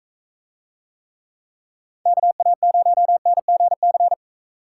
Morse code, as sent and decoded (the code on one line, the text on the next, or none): -.- .- ----- -. --. -.-.
KA0NGC